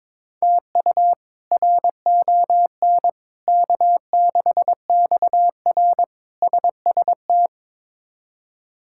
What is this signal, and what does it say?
Morse code, heard as a keyed tone, about 22 wpm, 700 Hz. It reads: TU RON K6XR SST
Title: K1USN SST QSO Practice - 22wpm